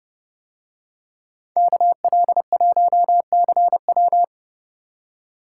Morse code, heard as a keyed tone, about 29 wpm, 700 Hz. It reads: KL1CW